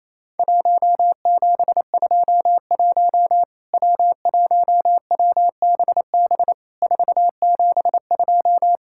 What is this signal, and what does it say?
Morse code, heard as a keyed tone, about 27 wpm, 700 Hz. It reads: 1721 W1WBB 472
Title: Worked All Europe DX Contest - QTC Exchange - 28wpm